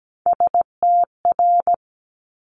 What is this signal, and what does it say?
Morse code, heard as a keyed tone, about 17 wpm, 700 Hz. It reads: STR
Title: Multi-letter Phonemes 17wpm